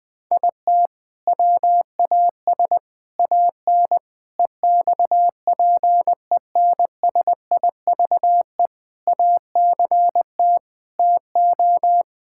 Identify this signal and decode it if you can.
Morse code, heard as a keyed tone, about 20 wpm, 700 Hz. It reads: IT WAS AN EXPENSIVE ACT TO
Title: Sentences from Top 1000 Words - Review Part 6 - 20wpm